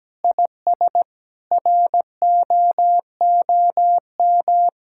Morse code, heard as a keyed tone, about 17 wpm, 700 Hz. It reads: IS ROOM
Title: Sentences from Top 1600 Words 17wpm